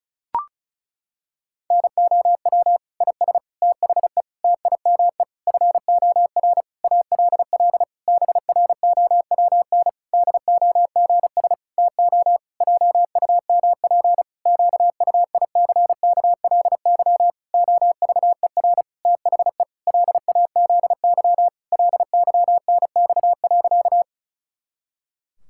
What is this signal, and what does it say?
Morse code, heard as a keyed tone, about 36 wpm, 700 Hz. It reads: NOW IS THE TIME FOR ALL BROWN DOGS TO JUMP QUICKLY OVER THE LAZY LYNX.
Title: Pangram Challenge v2 - 35wpm